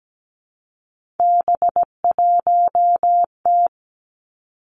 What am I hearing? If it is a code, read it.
Morse code, heard as a keyed tone, about 17 wpm, 700 Hz. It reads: B1T
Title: Three Challenging Letter-Number Combinations 17wpm